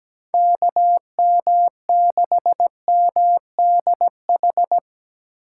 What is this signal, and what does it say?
Morse code, heard as a keyed tone, about 17 wpm, 700 Hz. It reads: KM6MDH
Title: USA Call Signs - 8x Character Spacing - 17wpm